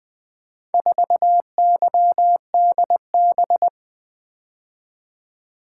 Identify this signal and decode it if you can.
Morse code, heard as a keyed tone, about 20 wpm, 700 Hz. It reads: 4YDB
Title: Four Letter-Number Combinations - 20wpm